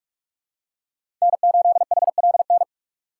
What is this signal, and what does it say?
Morse code, heard as a keyed tone, about 41 wpm, 700 Hz. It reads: N8HLN
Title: USA Call Signs - 5 Characters - 6x Character Spacing - 45wpm